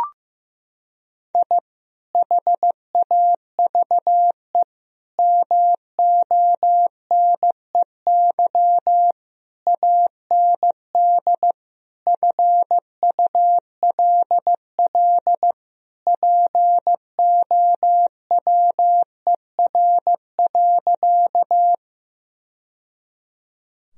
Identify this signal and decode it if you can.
Morse code, heard as a keyed tone, about 16 wpm, 700 Hz. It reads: I HAVE MONEY AND FULL POWER.
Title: Sentences from Top 1000 Words - Review Part 3 - 15wpm